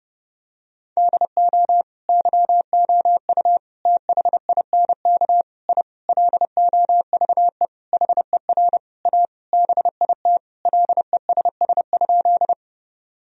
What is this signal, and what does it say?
Morse code, heard as a keyed tone, about 32 wpm, 700 Hz. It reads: DO YOU THINK I LOVE HER A BIT LESS?
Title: Sentences from Top 1000 Words - Review Part 2 - 30wpm